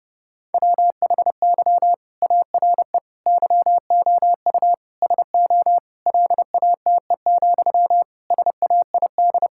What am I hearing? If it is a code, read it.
Morse code, heard as a keyed tone, about 30 wpm, 700 Hz. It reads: WHY ARE YOU SO LATE, SAID
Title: Sentences from Top 1000 Words - Review Part 6 - 30wpm